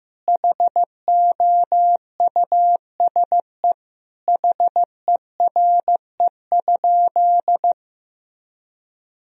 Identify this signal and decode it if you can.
Morse code, heard as a keyed tone, about 15 wpm, 700 Hz. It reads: HOUSE HERE?
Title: Sentences from Top 1000 Words - Review Part 1 - 15wpm